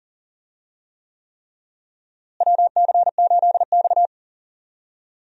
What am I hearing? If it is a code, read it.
Morse code, heard as a keyed tone, about 40 wpm, 700 Hz. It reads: WC8X
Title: USA Call Signs - 4 Characters - Repeated 3x - 40wpm